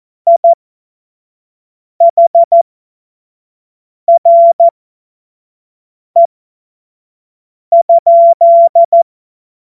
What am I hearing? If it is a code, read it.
Morse code, heard as a keyed tone, about 14 wpm, spaced as for 5 wpm, 680 Hz. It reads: IHRE?
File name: Morsekurs / Witze / Witz 10. August 2021, 25 F70